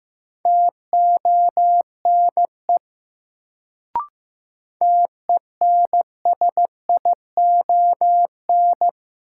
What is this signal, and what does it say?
Morse code, heard as a keyed tone, about 15 wpm, 700 Hz. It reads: TONE TENSION
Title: Sentences from Top 1600 Words - New Word Drill Alt - 15wpm